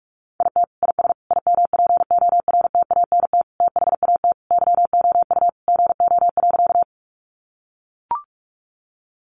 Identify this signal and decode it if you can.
Morse code, heard as a keyed tone, about 44 wpm, 700 Hz. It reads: IT IS IMPORTANT THAT YOU GO.
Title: Sentences from Top 600 Words 45wpm